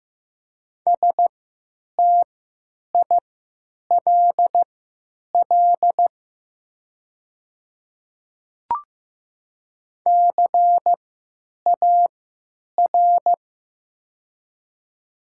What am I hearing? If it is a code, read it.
Morse code, heard as a keyed tone, about 15 wpm, spaced as for 9 wpm, 700 Hz. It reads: STILL CAR
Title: 3x Character Spacing - Top 300 Words - 15wpm